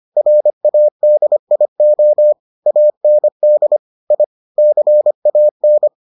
Morse code, heard as a keyed tone, about 25 wpm, 600 Hz. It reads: RADIO AND I CAN